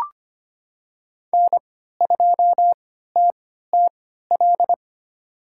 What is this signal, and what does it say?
Morse code, heard as a keyed tone, about 25 wpm, spaced as for 14 wpm, 700 Hz. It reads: N2TTL
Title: USA Call Signs - 5 Characters - 3x Character Spacing - 25wpm